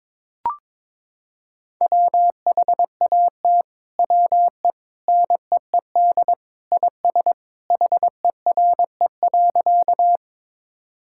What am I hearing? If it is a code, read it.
Morse code, heard as a keyed tone, about 22 wpm, 700 Hz. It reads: WHAT WE NEED IS HERE.
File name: practice / 22wpm / Sentences with 2-Word Phrases - 22wpm